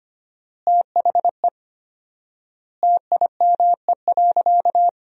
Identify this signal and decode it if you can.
Morse code, heard as a keyed tone, about 25 wpm, 700 Hz. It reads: THE TIME.
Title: 4x Word-Spacing - Sentences from Top 200 Words - 25wpm